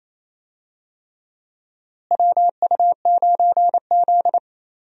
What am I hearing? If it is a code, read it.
Morse code, heard as a keyed tone, about 28 wpm, 700 Hz. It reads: WU9Z
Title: USA Call Signs - 4 Characters - Repeated 3x - 28wpm